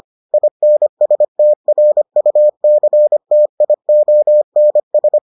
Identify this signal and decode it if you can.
Morse code, heard as a keyed tone, about 25 wpm, 600 Hz. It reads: INSTRUCTIONS